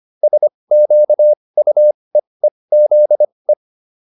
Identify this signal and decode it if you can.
Morse code, heard as a keyed tone, about 25 wpm, spaced as for 20 wpm, 600 Hz. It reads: SQUEEZE
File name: SS215 20